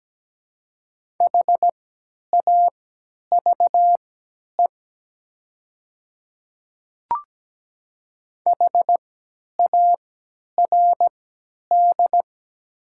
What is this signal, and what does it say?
Morse code, heard as a keyed tone, about 17 wpm, spaced as for 10 wpm, 700 Hz. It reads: HAVE HARD
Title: 3x Character Spacing - Top 500 Words - 17wpm